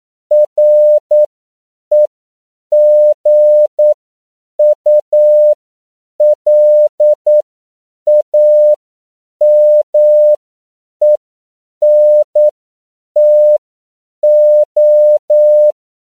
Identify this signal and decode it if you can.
Morse code, heard as a keyed tone, about 9 wpm, spaced as for 7 wpm, 600 Hz. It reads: REGULAMENTO
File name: CW / 10PPM